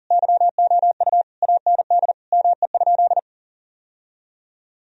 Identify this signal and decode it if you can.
Morse code, heard as a keyed tone, about 40 wpm, 700 Hz. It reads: YOU AND ME?
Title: Sentences from Top 1000 Words - Review Part 3 - 40wpm